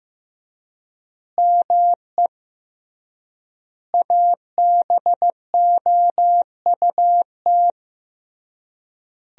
Morse code, heard as a keyed tone, about 16 wpm, 700 Hz. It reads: ME ABOUT
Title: 3x Word-Spacing - Sentences from Top 200 Words - 15wpm